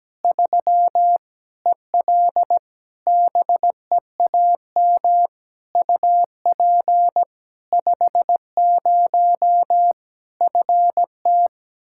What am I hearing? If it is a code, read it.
Morse code, heard as a keyed tone, about 17 wpm, 700 Hz. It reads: 3 EL BEAM UP 50 FT
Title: K1USN SST QSO Practice 17wpm